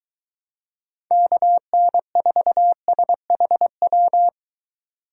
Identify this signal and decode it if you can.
Morse code, heard as a keyed tone, about 22 wpm, 700 Hz. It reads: KN4SHW